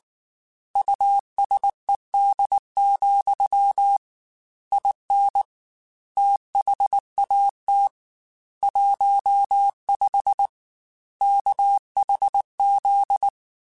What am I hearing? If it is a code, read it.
Morse code, heard as a keyed tone, about 19 wpm, 800 Hz. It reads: USED, IN THAT 15 KHZ
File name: Technical Subjects / Technical Subjects-mp3s / Sw Pwr Sup @18